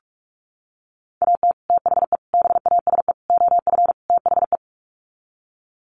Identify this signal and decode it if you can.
Morse code, heard as a keyed tone, about 44 wpm, 700 Hz. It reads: AT THE BASE OF THE
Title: Sets of 5 Words 45wpm